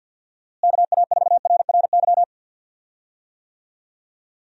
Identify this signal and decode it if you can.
Morse code, heard as a keyed tone, about 53 wpm, 700 Hz. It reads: KA4RRY
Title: USA Call Signs - 6 Characters - Repeated 2x - 50wpm